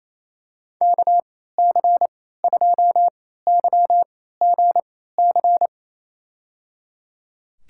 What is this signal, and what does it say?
Morse code, heard as a keyed tone, about 28 wpm, spaced as for 16 wpm, 700 Hz. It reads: KC2YGC
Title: USA Call Signs - 6 Characters - 3x Character Spacing - 28wpm